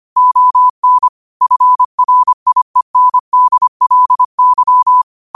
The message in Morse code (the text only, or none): ON FRIENDLY